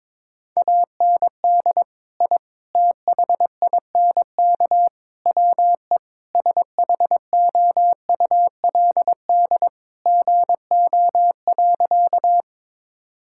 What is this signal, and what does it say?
Morse code, heard as a keyed tone, about 22 wpm, 700 Hz. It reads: AND I THINK WE SHOULD GO.
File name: practice / 22wpm / Sentences with 4-Word Phrases - 22wpm